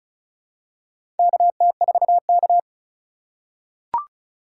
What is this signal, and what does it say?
Morse code, heard as a keyed tone, about 36 wpm, 700 Hz. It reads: KT4K
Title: USA Call Signs - 4x Character Spacing - 35wpm